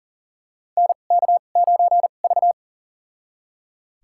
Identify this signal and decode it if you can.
Morse code, heard as a keyed tone, about 37 wpm, spaced as for 28 wpm, 690 Hz. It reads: NK9V
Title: USA Call Signs - 4 Characters - 2x Character Spacing - 40wpm